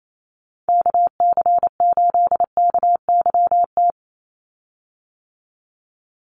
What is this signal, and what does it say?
Morse code, heard as a keyed tone, about 28 wpm, 700 Hz. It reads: KC8KYT
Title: USA Call Signs 28wpm